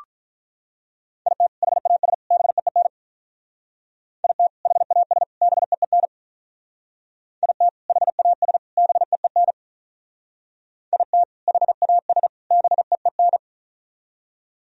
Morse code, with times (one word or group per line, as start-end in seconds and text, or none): 1.3-1.5 IT
1.6-2.1 HAS
2.3-2.9 BEEN
4.2-4.5 IT
4.7-5.2 HAS
5.4-6.1 BEEN
7.4-7.7 IT
7.9-8.6 HAS
8.8-9.5 BEEN
10.9-11.2 IT
11.5-12.3 HAS
12.5-13.4 BEEN